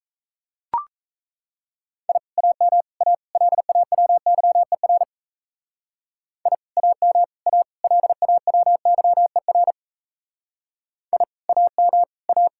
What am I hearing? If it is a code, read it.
Morse code, heard as a keyed tone, about 36 wpm, 700 Hz. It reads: I AM A LAWYER I AM A LAWYER I AM A